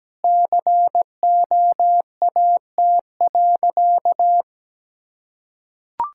Morse code, heard as a keyed tone, about 17 wpm, 700 Hz. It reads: COAT.
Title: Sentences from Top 1600 Words 17wpm